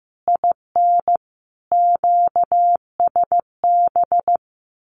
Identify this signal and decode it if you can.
Morse code, heard as a keyed tone, about 15 wpm, 700 Hz. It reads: IN QSB
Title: Common QSO Phrases 15wpm